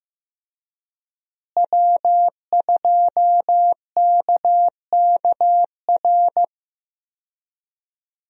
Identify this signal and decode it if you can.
Morse code, heard as a keyed tone, about 15 wpm, 700 Hz. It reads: W2KKR
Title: USA Call Signs - 5 Characters - Repeated 2x - 15wpm